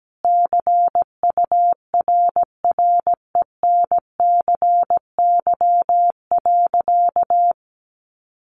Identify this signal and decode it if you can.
Morse code, heard as a keyed tone, about 17 wpm, 700 Hz. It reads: CURRENCY.
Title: Sentences from Top 1500 Words 17wpm